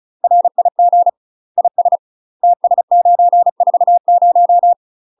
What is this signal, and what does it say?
Morse code, heard as a keyed tone, about 36 wpm, 700 Hz. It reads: RIG IS TS940
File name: UK MASTER QSOs TRAINING / 35WPM / UK MASTER QSOs 0013